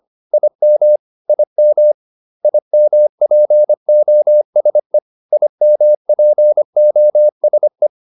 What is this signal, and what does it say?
Morse code, heard as a keyed tone, about 25 wpm, 600 Hz. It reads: IM IM IMPOSE IMPOSE